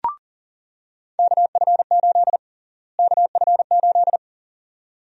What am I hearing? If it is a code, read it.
Morse code, heard as a keyed tone, about 39 wpm, 700 Hz. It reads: KF8 KF8
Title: USA Call Sign Prefixes - Repeated 2x - 40wpm